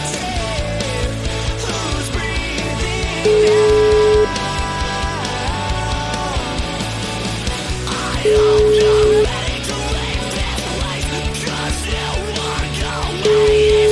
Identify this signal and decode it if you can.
Ringback tone (Europe, 425 Hz), with noise.